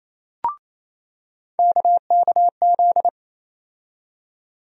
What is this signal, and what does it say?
Morse code, heard as a keyed tone, about 28 wpm, 700 Hz. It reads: KKZ
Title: USA Call Sign Postfixes - 3 Letters 28wpm